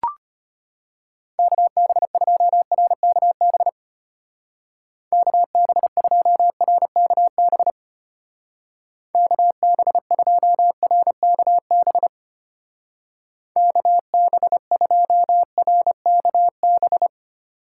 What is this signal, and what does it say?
Morse code, heard as a keyed tone, about 32 wpm, 700 Hz. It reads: KB2RKB KB2RKB KB2RKB KB2RKB